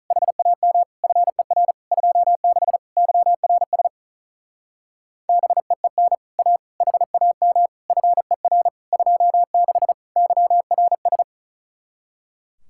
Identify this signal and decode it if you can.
Morse code, heard as a keyed tone, about 44 wpm, 700 Hz. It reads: HAM FER 26 YRS BEEN A HAM FER 26 YRS